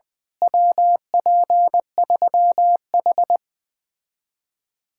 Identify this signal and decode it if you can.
Morse code, heard as a keyed tone, about 17 wpm, 700 Hz. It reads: WP3H
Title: Speed-Racing Common QSO Phrases - 17wpm